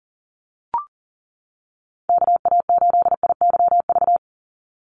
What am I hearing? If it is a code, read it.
Morse code, heard as a keyed tone, about 40 wpm, 700 Hz. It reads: KA8IYV
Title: USA Call Signs 40wpm